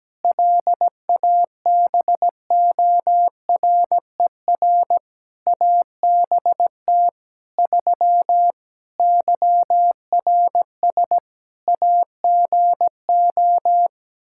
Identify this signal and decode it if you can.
Morse code, heard as a keyed tone, about 17 wpm, 700 Hz. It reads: LABORER ABT 3 YRS AGO
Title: Practice QSO - Exchange 3 - More Info - 17wpm